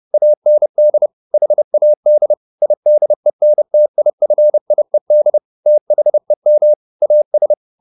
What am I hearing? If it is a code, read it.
Morse code, heard as a keyed tone, about 30 wpm, 600 Hz. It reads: AND HAD IDENTIFIED THEM AS